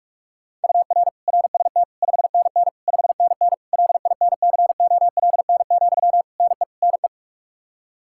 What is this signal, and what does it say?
Morse code, heard as a keyed tone, about 46 wpm, 700 Hz. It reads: UR RST 5NN 5NN LINCOLN, NE NE
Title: Practice QSO - Exchange 1 - Beginning - 45wpm